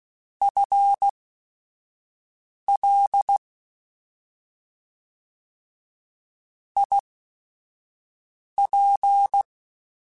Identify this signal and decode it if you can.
Morse code, heard as a keyed tone, about 16 wpm, spaced as for 5 wpm, 800 Hz. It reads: FL IP
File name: FL / FL mp3s / Fast FL @5